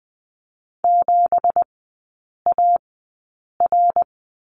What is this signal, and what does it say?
Morse code, heard as a keyed tone, about 20 wpm, spaced as for 8 wpm, 690 Hz. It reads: 7AR
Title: 166 Lesson 27e - Number 7 - Three Characters 20wpm